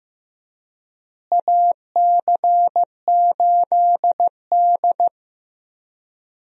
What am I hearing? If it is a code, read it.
Morse code, heard as a keyed tone, about 15 wpm, 700 Hz. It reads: AC8D
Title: USA Call Signs - 4 Characters - Repeated 3x - 15wpm